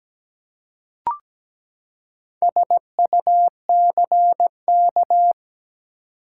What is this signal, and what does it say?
Morse code, heard as a keyed tone, about 17 wpm, 700 Hz. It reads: SUCK
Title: Four Letter Words - 17wpm